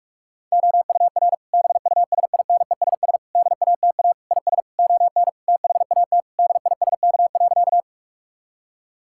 Morse code, heard as a keyed tone, about 43 wpm, 700 Hz. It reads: OUR BUSINESS DATA IS ON THAT DISK.
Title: Sentences from Top 1000 Words - Review Part 6 - 45wpm